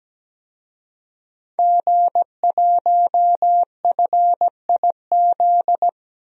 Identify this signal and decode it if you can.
Morse code, heard as a keyed tone, about 17 wpm, 700 Hz. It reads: G1FIZ
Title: DXCC-100 Call Signs - 5 Characters - 17wpm